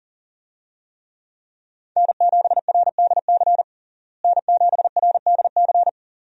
Morse code, heard as a keyed tone, about 40 wpm, 700 Hz. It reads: N7RDC N7RDC
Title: USA Call Signs - 5 Characters - Repeated 2x - 40wpm